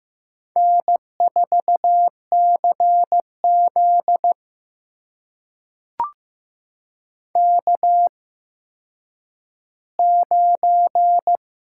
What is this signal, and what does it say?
Morse code, heard as a keyed tone, about 15 wpm, spaced as for 9 wpm, 700 Hz. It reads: N4CZ K9
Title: USA Call Signs - 4 Characters - 8x Character Spacing - 15wpm